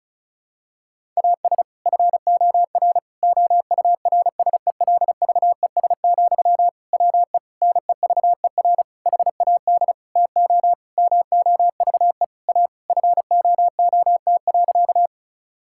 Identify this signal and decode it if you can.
Morse code, heard as a keyed tone, about 36 wpm, 700 Hz. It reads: AS FOR OURSELVES, WE NEVER HAD TO MOVE A FOOT.
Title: Sentences from Top 1600 Words 35wpm